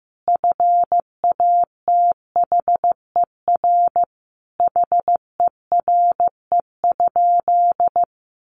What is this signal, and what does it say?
Morse code, heard as a keyed tone, about 15 wpm, 700 Hz. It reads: FATHER HERE?
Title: Sentences from Top 300 Words 15wpm